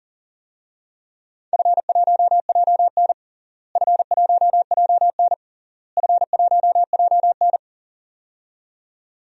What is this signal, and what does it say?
Morse code, heard as a keyed tone, about 37 wpm, 700 Hz. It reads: F1JN F1JN F1JN
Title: DXCC-100 Call Signs - 4 Characters - Repeated 3x - 40wpm